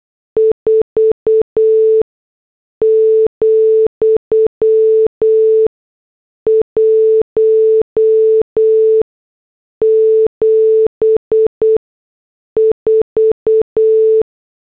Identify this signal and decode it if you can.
Morse code, heard as a keyed tone, about 8 wpm, spaced as for 6 wpm, 440 Hz. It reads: 4,174
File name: morse / headlines